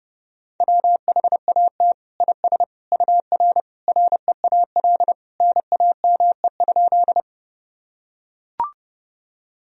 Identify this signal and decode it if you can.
Morse code, heard as a keyed tone, about 30 wpm, 700 Hz. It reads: WHAT IS UR REAL NAME?
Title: Sentences from Top 300 Words and Abbreviations 30wpm